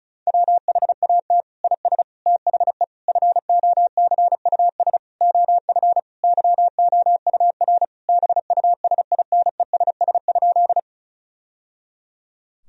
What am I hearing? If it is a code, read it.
Morse code, heard as a keyed tone, about 36 wpm, 700 Hz. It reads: WHAT IS THE FOCUS OF YOUR BUSINESS?
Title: Sentences from Top 1000 Words - Review Part 4 - 35wpm